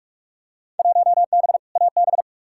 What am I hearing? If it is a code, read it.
Morse code, heard as a keyed tone, about 42 wpm, 700 Hz. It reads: 1B AB
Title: Field Day Exchanges - 45wpm